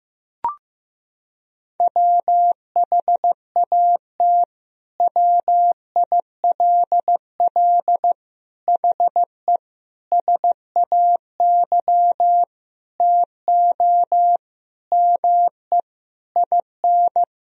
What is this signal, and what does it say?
Morse code, heard as a keyed tone, about 15 wpm, 700 Hz. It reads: WHAT WILL HE SAY TO ME IN
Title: Sentences from Top 1000 Words - Review Part 3 - 15wpm